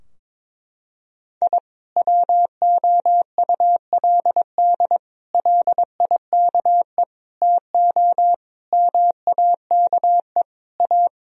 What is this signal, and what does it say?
Morse code, heard as a keyed tone, about 22 wpm, 700 Hz. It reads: I WOULD LIKE TO MAKE A
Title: Sentences from Top 1000 Words - Review Part 4 - 22wpm